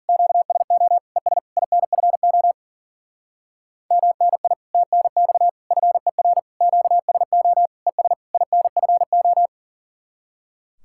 Morse code, heard as a keyed tone, about 48 wpm, 700 Hz. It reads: QSO ES INFO MNI TNX FER QSO ES INFO
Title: Speed-Racing Common QSO Phrases Encore - 40wpm